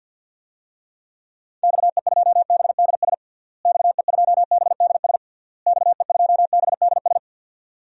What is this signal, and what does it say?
Morse code, heard as a keyed tone, about 51 wpm, 700 Hz. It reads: XE2BDS XE2BDS XE2BDS
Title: DXCC-100 Call Signs - 6 Characters - Repeated 3x - 50wpm